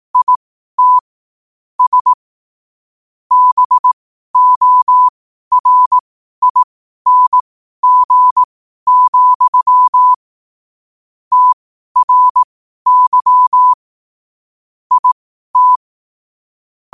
Morse code, heard as a keyed tone, about 18 wpm, spaced as for 12 wpm, 1000 Hz. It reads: IT S BORING, TRY IT
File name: qotdfiles / 12wpm